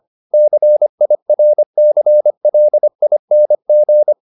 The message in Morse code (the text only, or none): CIRCLING